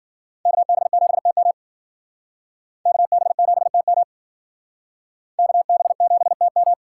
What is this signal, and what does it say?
Morse code, heard as a keyed tone, about 51 wpm, 700 Hz. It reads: KB7TM KB7TM KB7TM